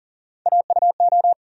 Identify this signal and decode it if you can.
Morse code, heard as a keyed tone, about 40 wpm, 700 Hz. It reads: AUO
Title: USA Call Sign Postfixes - 3 Letters 40wpm